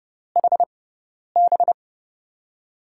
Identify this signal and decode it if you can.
Morse code, heard as a keyed tone, about 30 wpm, spaced as for 10 wpm, 700 Hz. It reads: HB
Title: USA Call Signs - 6x Character Spacing - 30wpm